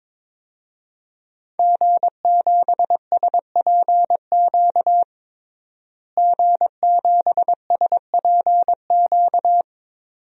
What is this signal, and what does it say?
Morse code, heard as a keyed tone, about 22 wpm, 700 Hz. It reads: G7SPQ G7SPQ
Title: DXCC-100 Call Signs - 5 Characters - Repeated 2x - 22wpm